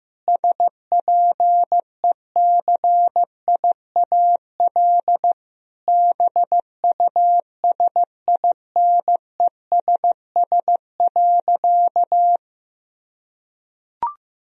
Morse code, heard as a keyed tone, about 15 wpm, 700 Hz. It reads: SPECIAL BUSINESS.
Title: Sentences from Top 1000 Words - Review Part 3 - 15wpm